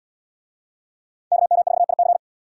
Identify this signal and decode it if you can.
Morse code, heard as a keyed tone, about 78 wpm, 700 Hz. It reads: KM4IC